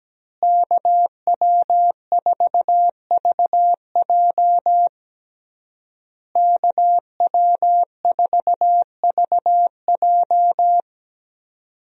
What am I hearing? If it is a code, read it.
Morse code, heard as a keyed tone, about 17 wpm, 700 Hz. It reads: KW4VJ KW4VJ
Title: USA Call Signs - 5 Characters - Repeated 3x - 17wpm